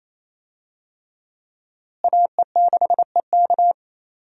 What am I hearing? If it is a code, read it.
Morse code, heard as a keyed tone, about 27 wpm, 700 Hz. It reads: AE6EK
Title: USA Call Signs - 5 Characters - Repeated 3x - 28wpm